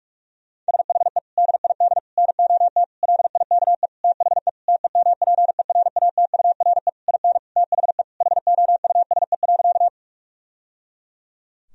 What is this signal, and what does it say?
Morse code, heard as a keyed tone, about 46 wpm, 700 Hz. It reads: SHE DID NOT LIKE THE TEMPERATURE IN THE HOUSE.
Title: Sentences from Top 1000 Words - Review Part 4 - 45wpm